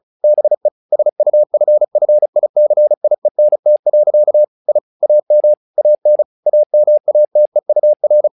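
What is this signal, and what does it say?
Morse code, heard as a keyed tone, about 36 wpm, 600 Hz. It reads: BE SUFFICIENT. I AM AN AMATEUR
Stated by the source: Morse practice text "Tower Trouble" sent at 35 WPM, from the SKCC - Straight Key Century Club learning center audio files.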